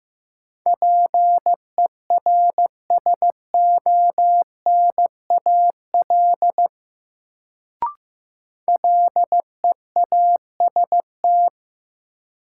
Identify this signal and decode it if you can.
Morse code, heard as a keyed tone, about 15 wpm, 700 Hz. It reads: PERSONAL LEAST
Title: Sentences from Top 700 Words - New Word Drill - 15wpm